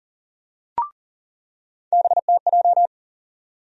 Morse code, heard as a keyed tone, about 40 wpm, 700 Hz. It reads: BTJ
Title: USA Call Sign Postfixes - 3 Letters 40wpm